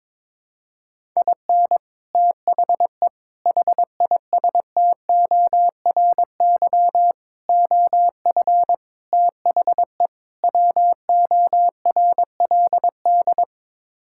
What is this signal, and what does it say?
Morse code, heard as a keyed tone, about 22 wpm, 700 Hz. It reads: IN THE HISTORY OF THE WORLD
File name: practice / 22wpm / Sets of 6 Words - 22wpm